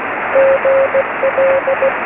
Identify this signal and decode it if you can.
Morse code, heard as a keyed tone, about 16 wpm, 530 Hz, with noise. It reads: GL
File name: sm7lcb / g3xdy 23cm